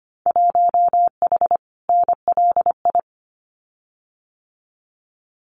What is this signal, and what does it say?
Morse code, heard as a keyed tone, about 25 wpm, 700 Hz. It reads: 1H NLI
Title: Winter Field Day Exchanges 25wpm